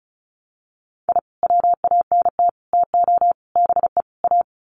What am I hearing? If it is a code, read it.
Morse code, heard as a keyed tone, about 34 wpm, 700 Hz. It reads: I WANT TO BE A
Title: Sets of 5 Words Encore 35wpm